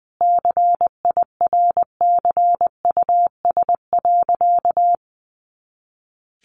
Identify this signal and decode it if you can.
Morse code, heard as a keyed tone, about 20 wpm, 700 Hz. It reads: CIRCUS.
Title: Easy Phrases 20wpm